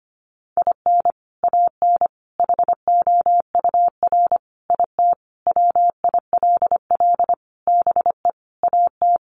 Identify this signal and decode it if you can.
Morse code, heard as a keyed tone, about 25 wpm, 700 Hz. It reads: IN AN HOUR IT WILL BE AT
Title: Sentences from Top 400 Words 25wpm